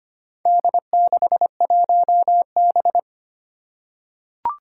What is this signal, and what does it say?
Morse code, heard as a keyed tone, about 25 wpm, 700 Hz. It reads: D61B
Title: Four Challenging Letter-Number Combinations 25wpm